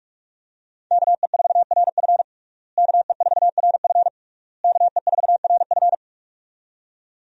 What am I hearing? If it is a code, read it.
Morse code, heard as a keyed tone, about 46 wpm, 700 Hz. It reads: KE4RF KE4RF KE4RF
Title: USA Call Signs - 5 Characters - Repeated 3x - 45wpm